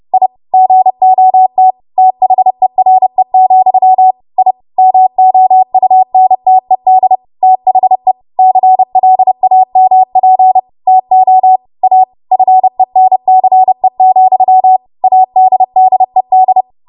Morse code, heard as a keyed tone, about 30 wpm, 750 Hz. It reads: I GOT THERE, I MOUNTED THE CLAMP TO A FENCE, ADDED
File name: Morse / 30WPM